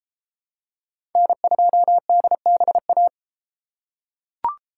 Morse code, heard as a keyed tone, about 32 wpm, 700 Hz. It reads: N2DBA